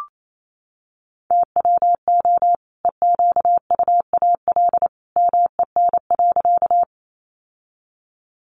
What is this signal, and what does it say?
Morse code, heard as a keyed tone, about 28 wpm, 700 Hz. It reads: TWO EQUAL MEN.
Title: Sentences from Top 1400 Words 28wpm